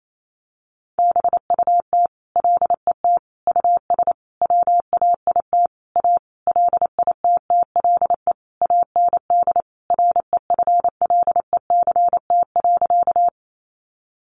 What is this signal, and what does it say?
Morse code, heard as a keyed tone, about 28 wpm, 700 Hz. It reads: BUT LET US WAIT A LITTLE AND REFLECT.
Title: Sentences from Top 1500 Words 28wpm